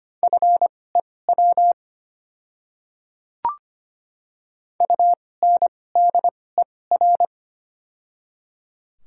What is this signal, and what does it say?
Morse code, heard as a keyed tone, about 25 wpm, spaced as for 18 wpm, 700 Hz. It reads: FEW UNDER
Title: 2x Character Spacing - Top 500 Words - 25wpm